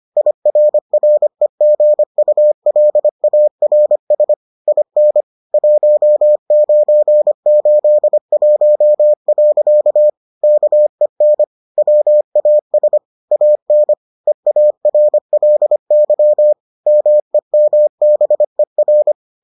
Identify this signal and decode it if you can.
Morse code, heard as a keyed tone, about 25 wpm, 600 Hz. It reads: IRREGULARS IN 1981. KEN WAS AN EARLY MEMBER